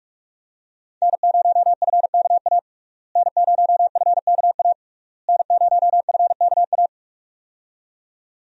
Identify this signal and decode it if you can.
Morse code, heard as a keyed tone, about 47 wpm, 700 Hz. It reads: N0FKA N0FKA N0FKA
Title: USA Call Signs - 5 Characters - Repeated 3x - 45wpm